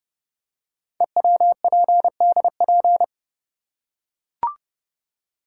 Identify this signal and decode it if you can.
Morse code, heard as a keyed tone, about 30 wpm, 700 Hz. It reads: EWPDP
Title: Code Groups - 5 Letters - 30wpm